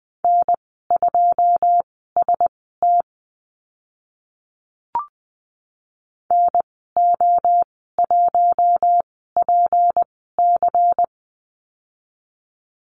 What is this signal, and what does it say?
Morse code, heard as a keyed tone, about 20 wpm, spaced as for 14 wpm, 700 Hz. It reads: N2ST NO1PC
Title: 193 Lesson 31g - Letter v - Call Signs 20wpm